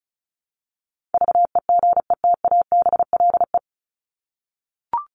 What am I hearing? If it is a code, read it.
Morse code, heard as a keyed tone, about 34 wpm, 700 Hz. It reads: VEGETABLE
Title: Top 1251-1500 Words 35wpm